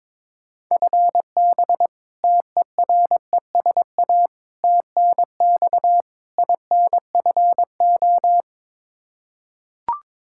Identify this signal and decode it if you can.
Morse code, heard as a keyed tone, about 22 wpm, 700 Hz. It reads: FB TERESA TNX INFO
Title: Practice QSO - Exchange 3 - More Info - 22wpm